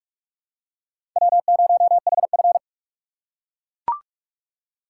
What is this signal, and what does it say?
Morse code, heard as a keyed tone, about 44 wpm, 700 Hz. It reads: W0HF